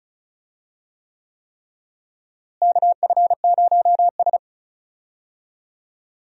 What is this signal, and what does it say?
Morse code, heard as a keyed tone, about 36 wpm, 700 Hz. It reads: KF0S
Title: USA Call Signs - 4 Characters - Repeated 3x - 35wpm